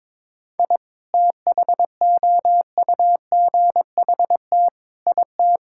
Morse code, heard as a keyed tone, about 22 wpm, 700 Hz. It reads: I THOUGHT IT
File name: Sets of 9 Words - 22wpm